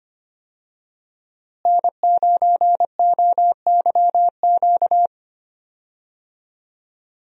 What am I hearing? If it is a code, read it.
Morse code, heard as a keyed tone, about 25 wpm, 700 Hz. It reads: N9OYQ
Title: USA Call Signs - 5 Characters - Repeated 3x - 25wpm